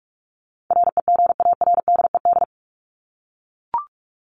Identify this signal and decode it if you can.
Morse code, heard as a keyed tone, about 44 wpm, 700 Hz. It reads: REGARDED
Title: ED Suffix - Top 100 Words with Base Words 45wpm